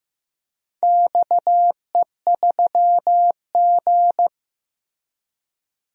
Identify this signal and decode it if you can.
Morse code, heard as a keyed tone, about 15 wpm, 700 Hz. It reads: XE3G
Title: DXCC-100 Call Signs - 4 Characters - Repeated 3x - 15wpm